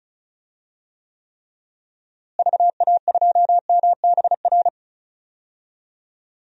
Morse code, heard as a keyed tone, about 36 wpm, 700 Hz. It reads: VA2MBR
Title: DXCC-100 Call Signs - 6 Characters - Repeated 3x - 35wpm